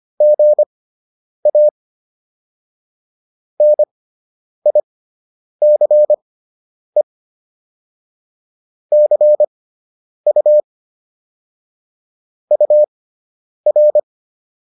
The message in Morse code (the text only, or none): GA NICE CU UR